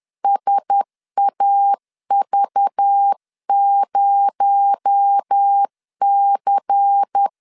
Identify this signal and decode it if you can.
Morse code, heard as a keyed tone, about 10 wpm, 800 Hz. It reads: SAV0C
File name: CW / DET / 1 6 1500C S C 10wpm 4252